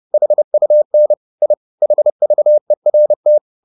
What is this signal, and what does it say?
Morse code, heard as a keyed tone, about 30 wpm, 600 Hz. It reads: HUN I HVERT
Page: Ferie 30 wpm